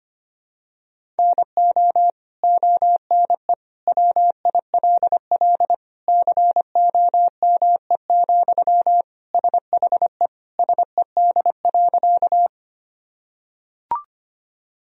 Morse code, heard as a keyed tone, about 25 wpm, 700 Hz. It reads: NO ONE WILL COME, SHE SED.
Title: Sentences from Top 100 Words and Abbreviations 25wpm